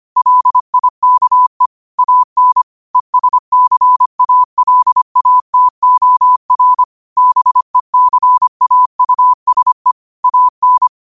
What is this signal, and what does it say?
Morse code, heard as a keyed tone, about 25 wpm, 1000 Hz. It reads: LIKE AN ESCALATOR BECAUSE AN
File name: qotdfiles / 25wpm